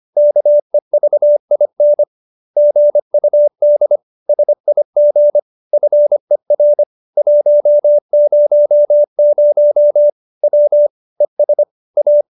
Morse code, heard as a keyed tone, about 25 wpm, 590 Hz. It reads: KEVIN GUD SIG FER 100 W ES A
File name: QSO306 25